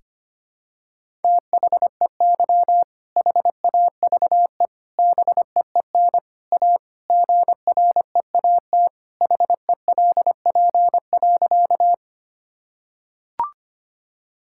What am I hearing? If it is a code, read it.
Morse code, heard as a keyed tone, about 25 wpm, 700 Hz. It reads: THEY HAVE BEEN A GREAT HELP.
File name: practice / 25wpm / Sentences with 3-Word Phrases - 25wpm